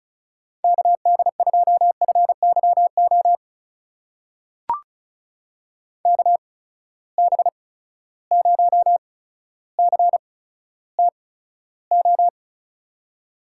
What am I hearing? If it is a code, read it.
Morse code, heard as a keyed tone, about 36 wpm, spaced as for 15 wpm, 700 Hz. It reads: KD2FYO KB0CTO
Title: USA Call Signs - 6 Characters - 8x Character Spacing - 35wpm